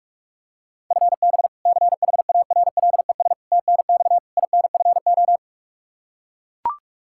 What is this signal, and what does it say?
Morse code, heard as a keyed tone, about 41 wpm, 700 Hz. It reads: FB CHARLES TNX INFO
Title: Practice QSO - Exchange 3 - More Info - 45wpm